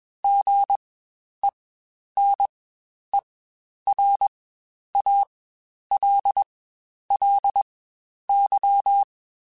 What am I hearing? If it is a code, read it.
Morse code, heard as a keyed tone, about 21 wpm, spaced as for 10 wpm, 800 Hz. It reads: GENERALLY
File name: Spaced Characters / Spaced Characters-mp3s / Weather5 1 sp @20